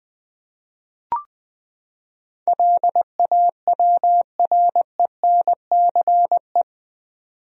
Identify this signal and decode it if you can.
Morse code, heard as a keyed tone, about 20 wpm, 700 Hz. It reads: LAWRENCE
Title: USA Common Names - 20wpm